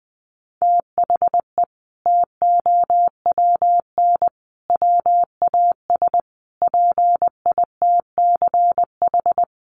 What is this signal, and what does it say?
Morse code, heard as a keyed tone, about 20 wpm, 700 Hz. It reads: THE TOWN WAS PITCH
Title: Sentences from Top 1300 Words 20wpm